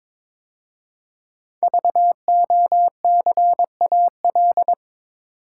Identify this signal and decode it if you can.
Morse code, heard as a keyed tone, about 22 wpm, 700 Hz. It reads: VOCAL
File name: 5 Letter Words - First 3 Letters Given - 22wpm